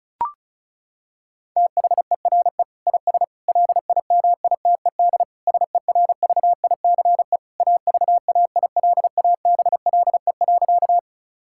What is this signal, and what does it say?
Morse code, heard as a keyed tone, about 36 wpm, 700 Hz. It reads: THERE IS LIMITED SERVICE AVAILABLE.
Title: Sentences from Top 1000 Words - Review Part 6 - 35wpm